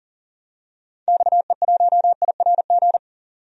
Morse code, heard as a keyed tone, about 37 wpm, 700 Hz. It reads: XE1IRG
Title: DXCC-100 Call Signs - 6 Characters - Repeated 2x - 40wpm